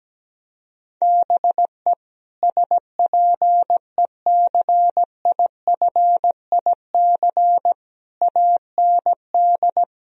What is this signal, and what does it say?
Morse code, heard as a keyed tone, about 17 wpm, 700 Hz. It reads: BE SPECIFIC AND
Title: Sentences from Top 1000 Words - Review Part 3 - 17wpm